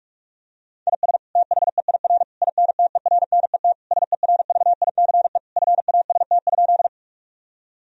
Morse code, heard as a keyed tone, about 46 wpm, 700 Hz. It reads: IS THEIR INTERNET SERVICE FAST?
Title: Sentences from Top 1000 Words - Review Part 4 - 45wpm